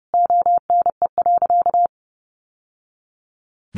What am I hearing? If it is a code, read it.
Morse code, heard as a keyed tone, about 30 wpm, 700 Hz. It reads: ONE.
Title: Sentences from Top 1100 Words 30wpm